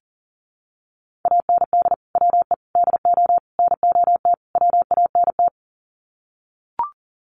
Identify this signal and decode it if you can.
Morse code, heard as a keyed tone, about 40 wpm, 700 Hz. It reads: AND WE DO NOT WANT
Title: Sets of 5 Words Encore2 40wpm